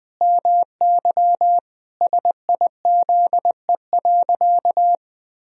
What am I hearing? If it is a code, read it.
Morse code, heard as a keyed tone, about 20 wpm, 700 Hz. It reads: MY SIZE.
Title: Sentences from Top 400 Words and Abbreviations 20wpm